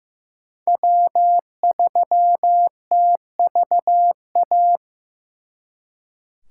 Morse code, heard as a keyed tone, about 18 wpm, 700 Hz. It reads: W3TVA
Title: Speed Racing - USA 1x3 Call Signs 15wpm